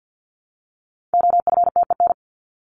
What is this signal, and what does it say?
Morse code, heard as a keyed tone, about 50 wpm, 700 Hz. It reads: OFTEN
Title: Top 300 Words Repeated 50wpm